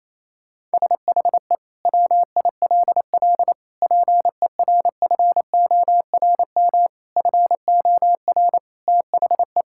Morse code, heard as a keyed tone, about 28 wpm, 700 Hz. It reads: SHE WILL PERFORM FOR THE
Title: Sentences from Top 1000 Words - Review Part 5 - 28wpm